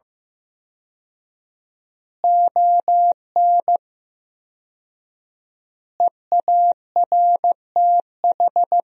Morse code, heard as a keyed tone, about 16 wpm, 700 Hz. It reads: ON EARTH
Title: 4x Word-Spacing - Sentences from Top 200 Words - 15wpm